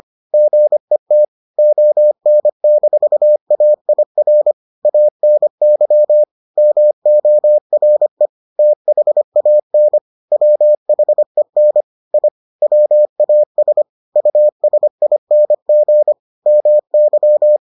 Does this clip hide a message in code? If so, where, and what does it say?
Yes: Morse code from 0.3 to 17.6 s, heard as a keyed tone, about 25 wpm, 600 Hz. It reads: GET ON-AIR ANY MORE THAN WHEN I WAS USING MY